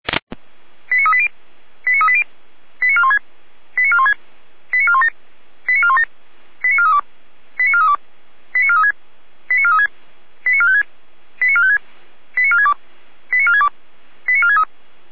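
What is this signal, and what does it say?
Five-tone selective call (ZVEI), heard as quick runs of five tones, tones of about 70 ms. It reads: 89380 89380 89416 89416 89417 89417 89432 89432 89436 89436 89456 89456 89462 89462 89463
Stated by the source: ZVEI five-tone call 89192 received with an SDR.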